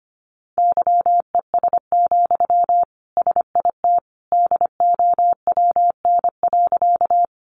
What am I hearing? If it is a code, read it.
Morse code, heard as a keyed tone, about 25 wpm, 700 Hz. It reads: YES, SIT DOWN.
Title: Sentences from Top 500 Words 25wpm